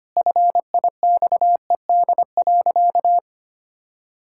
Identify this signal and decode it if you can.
Morse code, heard as a keyed tone, about 25 wpm, 700 Hz. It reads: FIXED.
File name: practice / 25wpm / Sentences with 3-Word Phrases - 25wpm